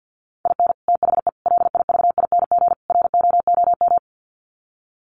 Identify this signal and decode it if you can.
Morse code, heard as a keyed tone, about 50 wpm, 700 Hz. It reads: IN THE LIVING ROOM
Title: Sets of 4 Words 50wpm